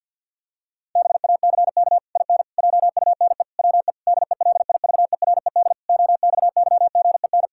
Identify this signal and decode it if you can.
Morse code, heard as a keyed tone, about 50 wpm, 700 Hz. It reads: BACK IN JUNE WE DELIVERED OXYGEN
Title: Pangram Challenge v2 - 50wpm